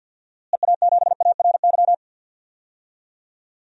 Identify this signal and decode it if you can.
Morse code, heard as a keyed tone, about 52 wpm, 700 Hz. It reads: EA7ARY
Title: DXCC-100 Call Signs - 6 Characters - Repeated 3x - 50wpm